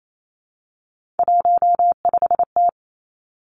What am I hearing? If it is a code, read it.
Morse code, heard as a keyed tone, about 28 wpm, 700 Hz. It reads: 15T
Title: Three Challenging Letter-Number Combinations 28wpm